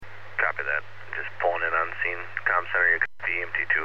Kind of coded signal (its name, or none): Emergency Alert System (SAME) tones